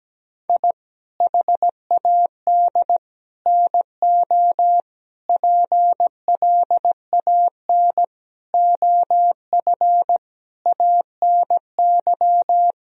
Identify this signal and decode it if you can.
Morse code, heard as a keyed tone, about 17 wpm, 700 Hz. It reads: I HAD NO PLAN OF ANY
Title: Sentences from Top 1000 Words - Review Part 2 - 17wpm